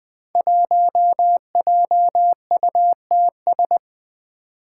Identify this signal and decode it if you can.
Morse code, heard as a keyed tone, about 20 wpm, 700 Hz. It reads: 1JUTS
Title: Five Challenging Letter-Number Combinations 20wpm